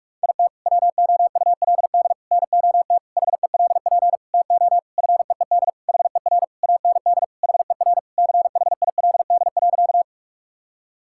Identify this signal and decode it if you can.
Morse code, heard as a keyed tone, about 46 wpm, 700 Hz. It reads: IT WOULD NOT HELP TO FEED HER AND HER CHILD.
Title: Sentences from Top 1600 Words 45wpm